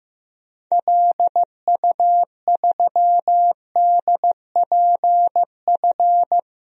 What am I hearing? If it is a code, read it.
Morse code, heard as a keyed tone, about 15 wpm, 700 Hz. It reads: LU3DPF
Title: DXCC-100 Call Signs - 6 Characters - 15wpm